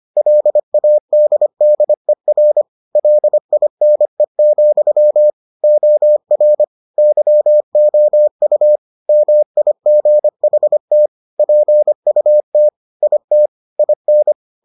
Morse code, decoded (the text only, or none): LADDER LINE, OR YOU MIGHT PUT IT IN